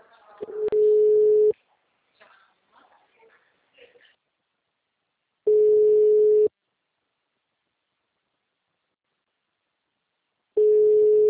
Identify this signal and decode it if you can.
Ringback tone (Europe, 425 Hz), with speech.
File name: out- -2014- - - .74